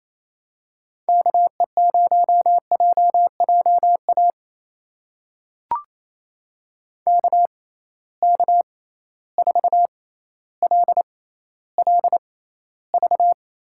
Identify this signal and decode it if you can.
Morse code, heard as a keyed tone, about 28 wpm, spaced as for 14 wpm, 700 Hz. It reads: KE0JJA KK4LLV
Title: USA Call Signs - 6 Characters - 6x Character Spacing - 28wpm